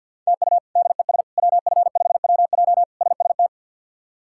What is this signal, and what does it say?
Morse code, heard as a keyed tone, about 53 wpm, 700 Hz. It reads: TU DES WF5WJ SST
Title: K1USN SST QSO Practice - 50wpm